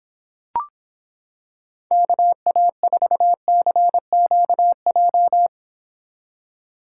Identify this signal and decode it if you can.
Morse code, heard as a keyed tone, about 26 wpm, 700 Hz. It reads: KA4CQJ